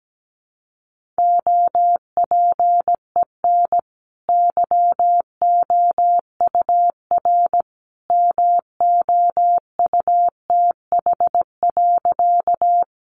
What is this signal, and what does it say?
Morse code, heard as a keyed tone, about 17 wpm, 700 Hz. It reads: OPEN YOUR MOUTH.
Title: Sentences from Top 1500 Words 17wpm